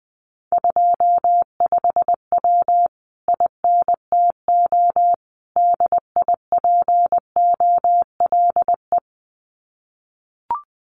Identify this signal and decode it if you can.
Morse code, heard as a keyed tone, about 20 wpm, 700 Hz. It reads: 25W INTO DIPOLE
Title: Common QSO Phrases Encore 20wpm